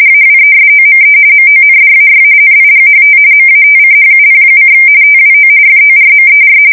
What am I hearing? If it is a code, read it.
RTTY, about 45.45 baud. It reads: AWESOME MONDAY NIGHT MATCHUP: TWO FORMER
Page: 170 Shift 2125/2295